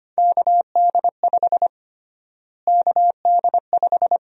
Morse code, heard as a keyed tone, about 25 wpm, 700 Hz. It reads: KD5 KD5
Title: USA Call Sign Prefixes - Repeated 2x - 25wpm